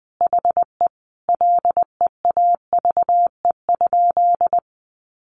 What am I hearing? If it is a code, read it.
Morse code, heard as a keyed tone, about 20 wpm, 700 Hz. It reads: HE LEAVE?
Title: Sentences from Top 400 Words 20wpm